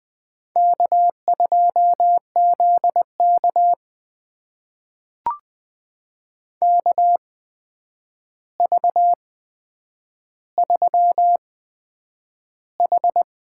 Morse code, heard as a keyed tone, about 20 wpm, spaced as for 9 wpm, 700 Hz. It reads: K2ZK KV3H
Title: USA Call Signs - 4 Characters - 8x Character Spacing - 20wpm